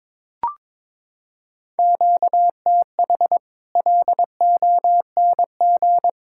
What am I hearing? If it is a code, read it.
Morse code, heard as a keyed tone, about 22 wpm, 700 Hz. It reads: QTH LONG
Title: Practice QSO - Exchange 1 - Beginning - 22wpm